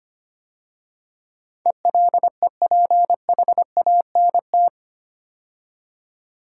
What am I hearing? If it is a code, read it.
Morse code, heard as a keyed tone, about 25 wpm, 690 Hz. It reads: ELEPHANT
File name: 8 Letter Words - First 3 Letters Given - 25wpm